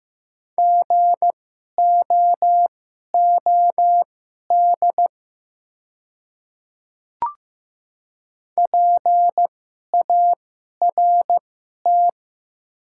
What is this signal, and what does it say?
Morse code, heard as a keyed tone, about 15 wpm, spaced as for 11 wpm, 700 Hz. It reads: GOOD PART
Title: 2x Character Spacing - Top 300 Words - 15wpm